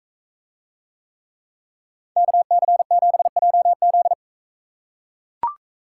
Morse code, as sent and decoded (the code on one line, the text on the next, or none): -.- -.-. --... .--- --..
KC7JZ